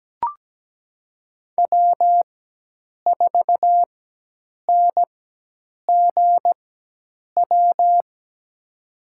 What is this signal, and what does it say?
Morse code, heard as a keyed tone, about 17 wpm, spaced as for 8 wpm, 700 Hz. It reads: W4NGW
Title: USA Call Signs - 4x Character Spacing - 17wpm